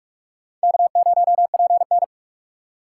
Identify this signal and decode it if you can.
Morse code, heard as a keyed tone, about 61 wpm, spaced as for 53 wpm, 700 Hz. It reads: K0PN
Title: Speed Racing - USA 1x2 Call Signs 45wpm